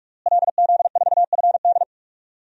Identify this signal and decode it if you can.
Morse code, heard as a keyed tone, about 46 wpm, 700 Hz. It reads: LZ4FD
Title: DXCC-100 Call Signs - 5 Characters - Repeated 3x - 45wpm